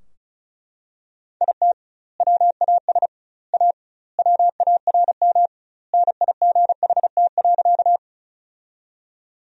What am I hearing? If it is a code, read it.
Morse code, heard as a keyed tone, about 34 wpm, 700 Hz. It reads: IT WAS A WARM NIGHT.
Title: 2x Word-Spacing - Sentences from Top 500 Words - 35wpm